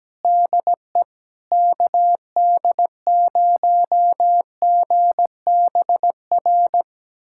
Morse code, heard as a keyed tone, about 17 wpm, 700 Hz. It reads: DE KD0GBR
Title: USA Callsign DE Callsign 17wpm